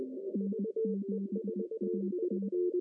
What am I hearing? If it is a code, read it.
Morse code, heard as a keyed tone, about 20 wpm, 200 Hz. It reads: N7AT
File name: CW / cqww-cw-k5tr